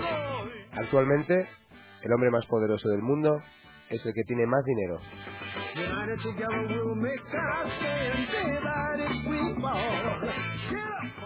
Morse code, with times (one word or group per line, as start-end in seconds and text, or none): none